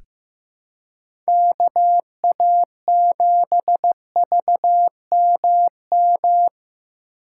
Morse code, heard as a keyed tone, about 15 wpm, 700 Hz. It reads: KA7VMM
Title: USA Call Signs - 6 Characters - 6x Character Spacing - 15wpm